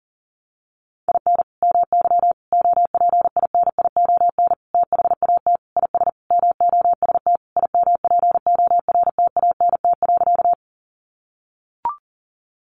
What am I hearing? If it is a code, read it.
Morse code, heard as a keyed tone, about 40 wpm, 700 Hz. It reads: IN MY OPINION THAT IS MOST IMPORTANT.
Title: Sentences from Top 1200 Words 40wpm